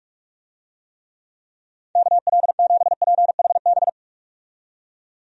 Morse code, heard as a keyed tone, about 46 wpm, 700 Hz. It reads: KL7PHB
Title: DXCC-100 Call Signs - 6 Characters - 45wpm